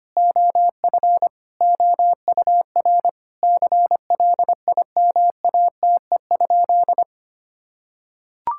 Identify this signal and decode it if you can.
Morse code, heard as a keyed tone, about 25 wpm, 700 Hz. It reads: OF OUR CLIMATE?
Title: Sentences from Top 1600 Words 25wpm